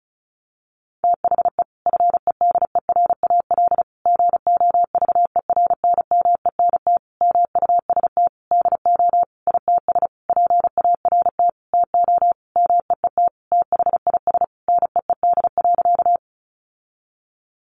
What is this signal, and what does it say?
Morse code, heard as a keyed tone, about 34 wpm, 700 Hz. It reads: THE FEDERAL GOVERNMENT MUST DO ITS PART TO MEET THIS NEED.
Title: Sentences from Top 1100 Words 35wpm